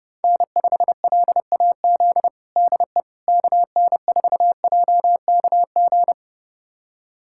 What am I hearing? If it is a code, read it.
Morse code, heard as a keyed tone, about 30 wpm, 700 Hz. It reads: N5LAZ DE KN4JKG
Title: USA Callsign DE Callsign 30wpm